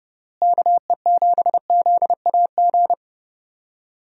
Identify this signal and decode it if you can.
Morse code, heard as a keyed tone, about 30 wpm, 700 Hz. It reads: KE7ZAG
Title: USA Call Signs - 6 Characters - Repeated 3x - 30wpm